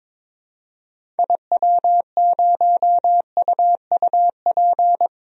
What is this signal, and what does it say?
Morse code, heard as a keyed tone, about 22 wpm, 700 Hz. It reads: IW0UUP
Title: DXCC-100 Call Signs - 6 Characters - 22wpm